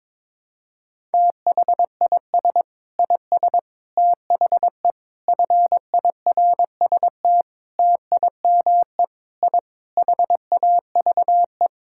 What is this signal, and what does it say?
Morse code, heard as a keyed tone, about 22 wpm, 700 Hz. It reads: THIS IS THE FIRST TIME I HAVE
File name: practice / 22wpm / Sentences with 4-Word Phrases - 22wpm